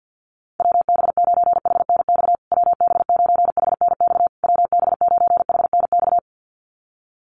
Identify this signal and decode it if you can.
Morse code, heard as a keyed tone, about 50 wpm, 700 Hz. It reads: WB9HNX WB9HNX WB9HNX
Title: USA Call Signs Repeated 3x 50wpm